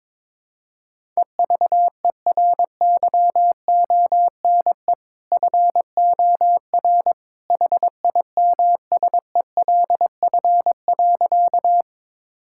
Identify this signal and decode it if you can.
Morse code, heard as a keyed tone, about 22 wpm, 700 Hz. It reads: EVERYONE FOR HIMSELF.
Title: Sentences from Top 1600 Words 22wpm